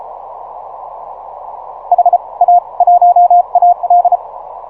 Morse code, heard as a keyed tone, about 32 wpm, spaced as for 25 wpm, 710 Hz. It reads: HA1AD
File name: simple-file-list / 5A1AL 6MCW